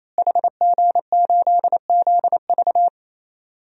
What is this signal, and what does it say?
Morse code, heard as a keyed tone, about 28 wpm, 700 Hz. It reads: HG8ZV
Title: DXCC-100 Call Signs - 5 Characters - Repeated 3x - 28wpm